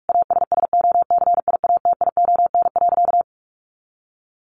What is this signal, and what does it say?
Morse code, heard as a keyed tone, about 44 wpm, 700 Hz. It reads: ASSOCIATION.
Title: Sentences from Top 1200 Words 45wpm